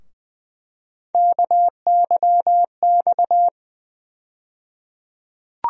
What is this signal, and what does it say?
Morse code, heard as a keyed tone, about 20 wpm, 700 Hz. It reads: KYX
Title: USA Call Sign Postfixes - 3 Letters 20wpm